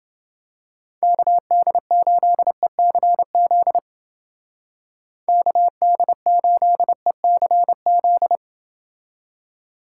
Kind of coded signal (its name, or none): Morse code